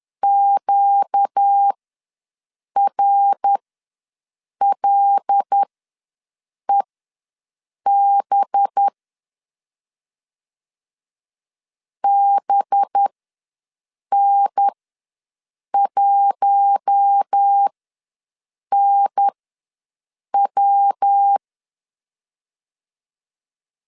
Morse code, heard as a keyed tone, about 10 wpm, spaced as for 6 wpm, 800 Hz. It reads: QRLEB BN1NW
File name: LEZIONI / 5 LBVC / 5 LB ws 10wpm 3616